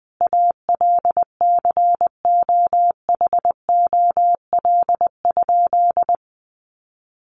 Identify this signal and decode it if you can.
Morse code, heard as a keyed tone, about 20 wpm, 700 Hz. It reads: ALCOHOL?
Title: Sentences from Top 1400 Words 20wpm